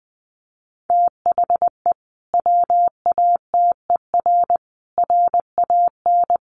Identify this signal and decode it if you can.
Morse code, heard as a keyed tone, about 20 wpm, 700 Hz. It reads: THE WATER RAN
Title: Sentences from Top 500 Words 20wpm